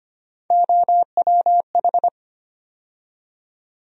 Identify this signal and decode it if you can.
Morse code, heard as a keyed tone, about 25 wpm, 700 Hz. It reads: OWH
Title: USA Call Sign Postfixes - 3 Letters 25wpm